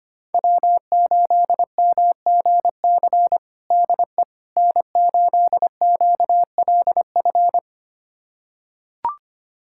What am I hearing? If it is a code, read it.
Morse code, heard as a keyed tone, about 25 wpm, 700 Hz. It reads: W8MGC DE N8QLF
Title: USA Callsign DE Callsign - 5 Characters - 25wpm